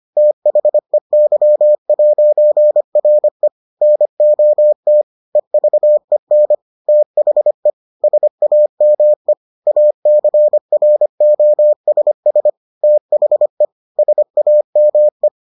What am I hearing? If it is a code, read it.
Morse code, heard as a keyed tone, about 25 wpm, 600 Hz. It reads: THEY'RE NOT EVEN THE SAME ACROSS THE SAME